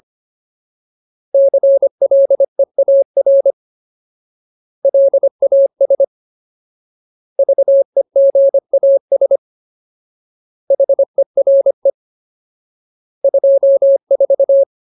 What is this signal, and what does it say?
Morse code, heard as a keyed tone, about 25 wpm, 550 Hz. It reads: CLEAR LAS VEGAS HERE 24
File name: Ultimate Word List / 25WPM Ultimate